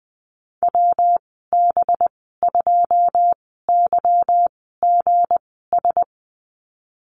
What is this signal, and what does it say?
Morse code, heard as a keyed tone, about 20 wpm, spaced as for 15 wpm, 700 Hz. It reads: WB2YGS
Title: 193 Lesson 31g - Letter v - Call Signs 20wpm